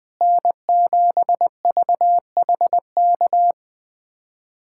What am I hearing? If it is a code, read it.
Morse code, heard as a keyed tone, about 20 wpm, 700 Hz. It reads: N7VHK
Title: USA Callsign DE Callsign 20wpm